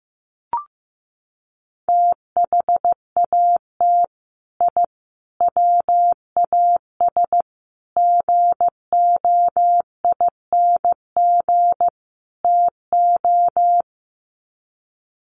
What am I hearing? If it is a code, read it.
Morse code, heard as a keyed tone, about 15 wpm, 700 Hz. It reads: THAT I WAS GOING TO
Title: Sets of 5 Words Encore 15wpm